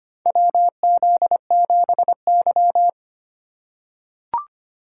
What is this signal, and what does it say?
Morse code, heard as a keyed tone, about 25 wpm, 700 Hz. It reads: WZ7Y
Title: USA Call Signs - 4 Characters - 3x Character Spacing - 25wpm